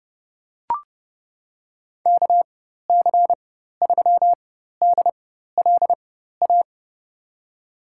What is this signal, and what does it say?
Morse code, heard as a keyed tone, about 30 wpm, spaced as for 14 wpm, 700 Hz. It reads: KC3DLA
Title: USA Call Signs - 6 Characters - 4x Character Spacing - 30wpm